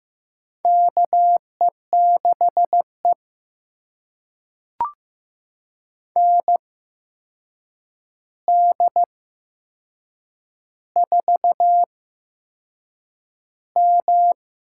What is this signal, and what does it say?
Morse code, heard as a keyed tone, about 15 wpm, spaced as for 6 wpm, 700 Hz. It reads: KE6E ND4M
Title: USA Call Signs - 4 Characters - 8x Character Spacing - 15wpm